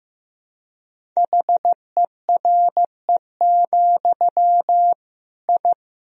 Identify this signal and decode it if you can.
Morse code, heard as a keyed tone, about 15 wpm, 700 Hz. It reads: HERE, I
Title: Sentences from Top 1000 Words - Review Part 1 - 15wpm